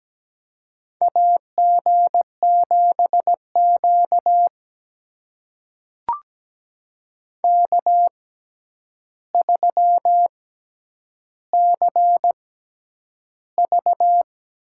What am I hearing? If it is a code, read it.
Morse code, heard as a keyed tone, about 17 wpm, spaced as for 9 wpm, 700 Hz. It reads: AG7Q K3CV
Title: USA Call Signs - 4 Characters - 6x Character Spacing - 17wpm